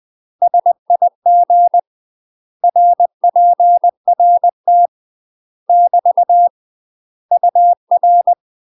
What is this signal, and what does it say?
Morse code, heard as a keyed tone, about 20 wpm, 700 Hz. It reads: SIG RPRT <BT> UR
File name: UK MASTER QSOs TRAINING / 20WPM / UK MASTER QSOs 0003